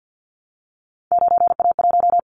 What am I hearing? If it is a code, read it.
Morse code, heard as a keyed tone, about 50 wpm, 700 Hz. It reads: 9A1
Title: Callsign Prefixes for Top 25 DXCC - 50wpm